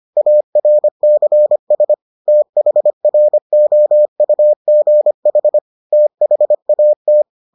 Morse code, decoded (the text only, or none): ARCS THROUGH THAT